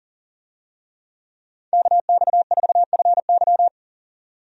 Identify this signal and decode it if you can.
Morse code, heard as a keyed tone, about 40 wpm, 700 Hz. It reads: KX4FY
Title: USA Call Signs - 5 Characters - 40wpm